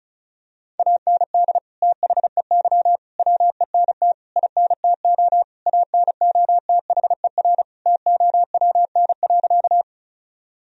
Morse code, heard as a keyed tone, about 36 wpm, 700 Hz. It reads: AND THEY WENT INTO ANOTHER TOWN.
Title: Sentences from Top 1000 Words - Review Part 2 - 35wpm